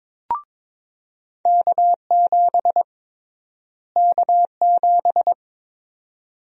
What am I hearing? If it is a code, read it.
Morse code, heard as a keyed tone, about 22 wpm, 700 Hz. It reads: K7 K7
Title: USA Call Sign Prefixes - Repeated 2x - 22wpm